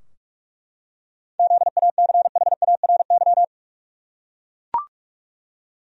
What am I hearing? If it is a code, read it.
Morse code, heard as a keyed tone, about 43 wpm, 700 Hz. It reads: ZACHARY
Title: USA Common Names - 45wpm